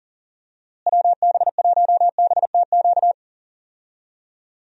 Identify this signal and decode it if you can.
Morse code, heard as a keyed tone, about 40 wpm, 700 Hz. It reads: WB1BTQ
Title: USA Call Signs - 6 Characters - Repeated 3x - 40wpm